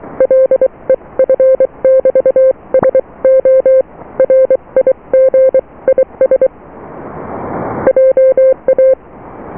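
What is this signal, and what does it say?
Morse code, heard as a keyed tone, about 23 wpm, spaced as for 20 wpm, 520 Hz. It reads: LEF <BT> SO RIG IS JA
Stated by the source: Morse QSO on 14 MHz with a German station, received at RST 579.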